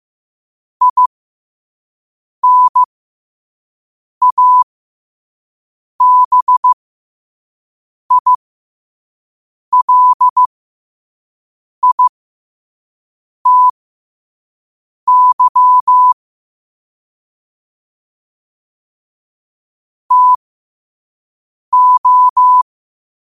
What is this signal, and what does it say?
Morse code, heard as a keyed tone, about 15 wpm, spaced as for 5 wpm, 1000 Hz. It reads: INABILITY TO
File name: qotdfiles / 5wpm